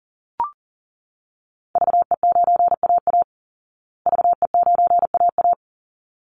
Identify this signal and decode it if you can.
Morse code, heard as a keyed tone, about 40 wpm, 700 Hz. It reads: VE9AA VE9AA
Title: Call Signs Repeated 3x - 40wpm